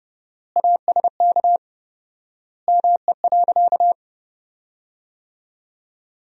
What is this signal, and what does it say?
Morse code, heard as a keyed tone, about 32 wpm, 700 Hz. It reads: ASK ME.
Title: 4x Word-Spacing - Sentences from Top 500 Words - 30wpm